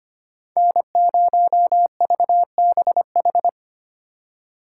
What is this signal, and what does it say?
Morse code, heard as a keyed tone, about 32 wpm, 700 Hz. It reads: N0VBH